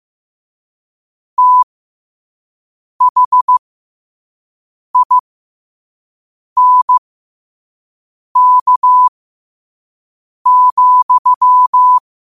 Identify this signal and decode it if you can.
Morse code, heard as a keyed tone, about 15 wpm, spaced as for 5 wpm, 1000 Hz. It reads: THINK,
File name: qotdfiles / 5wpm